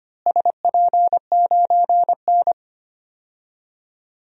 Morse code, heard as a keyed tone, about 25 wpm, 700 Hz. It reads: SP9N
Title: DXCC-100 Call Signs - 4 Characters - Repeated 2x - 25wpm